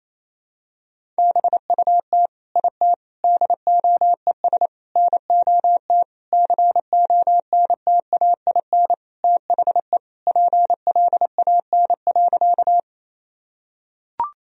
Morse code, heard as a keyed tone, about 28 wpm, 700 Hz. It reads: BUT IT DOES NOT CONTAIN THE PLAN.
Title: Sentences from Top 1000 Words - Review Part 2 - 28wpm